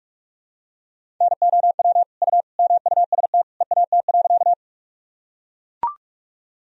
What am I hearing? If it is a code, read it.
Morse code, heard as a keyed tone, about 43 wpm, 700 Hz. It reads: NOW U MUST EAT.
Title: Sentences from Top 300 Words and Abbreviations 45wpm